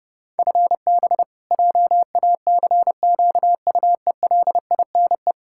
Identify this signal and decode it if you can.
Morse code, heard as a keyed tone, about 30 wpm, 700 Hz. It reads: FB JACQUELINE
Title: Practice QSO - Exchange 3 - More Info - 30wpm